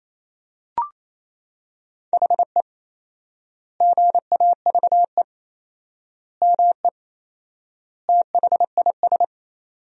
Morse code, heard as a keyed tone, about 28 wpm, 700 Hz. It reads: HE GAVE ME THIS
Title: 4x Word-Spacing - Sentences from Top 500 Words - 28wpm